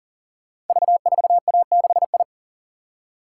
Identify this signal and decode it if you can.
Morse code, heard as a keyed tone, about 40 wpm, 690 Hz. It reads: V4A6I
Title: Code Groups - 5 Letter:Numbers - 40wpm